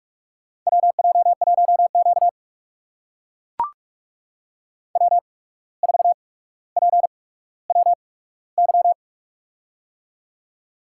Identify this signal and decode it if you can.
Morse code, heard as a keyed tone, about 44 wpm, spaced as for 18 wpm, 700 Hz. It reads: WJ1Q W4PWY
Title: USA Call Signs - 8x Character Spacing - 45wpm